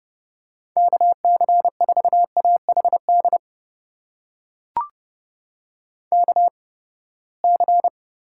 Morse code, heard as a keyed tone, about 30 wpm, spaced as for 21 wpm, 700 Hz. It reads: KC4AHD KC
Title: USA Call Signs - 6 Characters - 8x Character Spacing - 30wpm